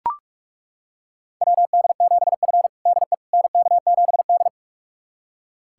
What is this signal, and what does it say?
Morse code, heard as a keyed tone, about 41 wpm, 700 Hz. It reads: WD7F DE NK7D
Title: USA Callsign DE Callsign - 4 Characters - 45wpm